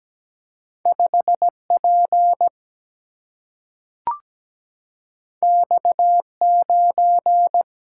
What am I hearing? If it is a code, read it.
Morse code, heard as a keyed tone, about 17 wpm, 700 Hz. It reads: 5 P X 9
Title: Single-Letter Single-Number 17wpm